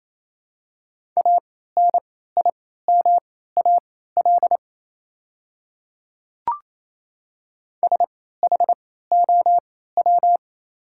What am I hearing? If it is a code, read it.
Morse code, heard as a keyed tone, about 28 wpm, spaced as for 16 wpm, 700 Hz. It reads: ANIMAL SHOW
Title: 3x Character Spacing - Top 200 Words - 28wpm